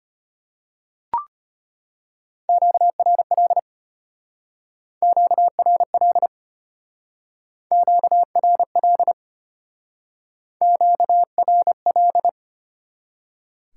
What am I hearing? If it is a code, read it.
Morse code, heard as a keyed tone, about 31 wpm, 700 Hz. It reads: QRL QRL QRL QRL
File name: practice / 25wpm / Speed-Racing Top 20 QSO Elements - 25wpm